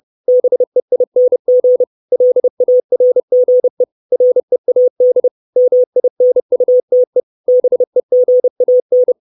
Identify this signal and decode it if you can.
Morse code, heard as a keyed tone, about 30 wpm, 500 Hz. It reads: BEING LARGE READ MINUTE BEGAN
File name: 632 Words 30WPM